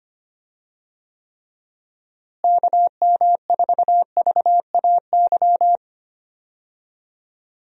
Morse code, heard as a keyed tone, about 25 wpm, 700 Hz. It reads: KM4VAY
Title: USA Call Signs - 6 Characters - Repeated 2x - 25wpm